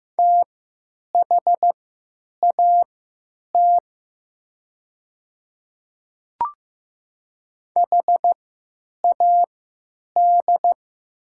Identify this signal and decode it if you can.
Morse code, heard as a keyed tone, about 15 wpm, spaced as for 9 wpm, 700 Hz. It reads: THAT HAD
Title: 3x Character Spacing - Top 100 Words - 15wpm